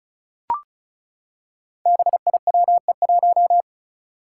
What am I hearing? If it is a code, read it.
Morse code, heard as a keyed tone, about 36 wpm, 700 Hz. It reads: BIWE1
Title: Five Challenging Letter-Number Combinations 35wpm